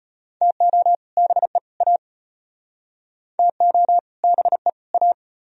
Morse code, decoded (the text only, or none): TO BE A TO BE A